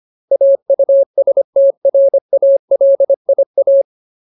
Morse code, heard as a keyed tone, about 25 wpm, 550 Hz. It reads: AUSTRALIA
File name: Ultimate Word List / 25WPM Ultimate